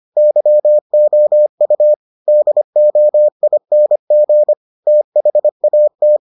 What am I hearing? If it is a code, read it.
Morse code, heard as a keyed tone, about 25 wpm, 600 Hz. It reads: YOU DOING THAT